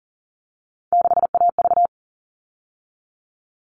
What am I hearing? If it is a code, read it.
Morse code, heard as a keyed tone, about 40 wpm, 700 Hz. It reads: 6AV
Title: All Cut Numbers - 3-Digit 40wpm